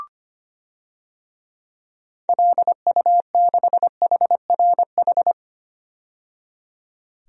Morse code, heard as a keyed tone, about 25 wpm, 700 Hz. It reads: LU6HRH
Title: DXCC-100 Call Signs - 6 Characters - 25wpm